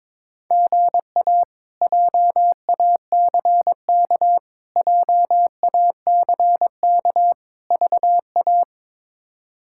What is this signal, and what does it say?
Morse code, heard as a keyed tone, about 22 wpm, 700 Hz. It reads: GA JACK JACK VA
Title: K1USN SST QSO Practice - 22wpm